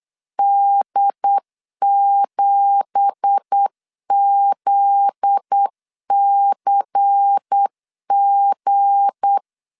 Morse code, heard as a keyed tone, about 8 wpm, 800 Hz. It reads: D7ZCG
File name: LEZIONI / 10 SIGNS / 10C DOT 8wpm 3649